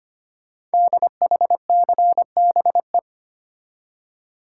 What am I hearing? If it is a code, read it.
Morse code, heard as a keyed tone, about 25 wpm, 700 Hz. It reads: DHCBE
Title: Code Groups - 5 Letters - 25wpm